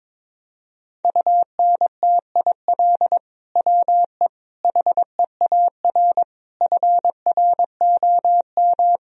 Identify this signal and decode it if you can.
Morse code, heard as a keyed tone, about 21 wpm, 700 Hz. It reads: UNTIL WE HEAR FROM
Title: Sentences from Top 1000 Words - Review Part 1 - 22wpm